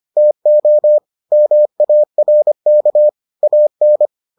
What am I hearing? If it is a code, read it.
Morse code, heard as a keyed tone, about 25 wpm, 600 Hz. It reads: TO MARK AN